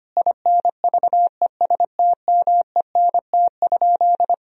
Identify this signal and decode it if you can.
Morse code, heard as a keyed tone, about 25 wpm, 700 Hz. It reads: INVESTMENT?
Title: Sentences from Top 1000 Words - Review Part 4 - 25wpm